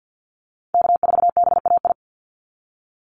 Morse code, heard as a keyed tone, about 48 wpm, 700 Hz. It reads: K4BAI
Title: Call Signs Repeated 3x - 50wpm